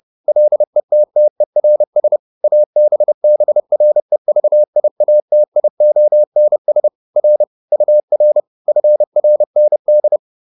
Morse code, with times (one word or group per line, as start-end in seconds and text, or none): 0.3-2.2 LETTERS
2.4-6.9 ABBREVIATIONS
7.2-7.4 R
7.7-8.4 UR
8.7-10.2 FRND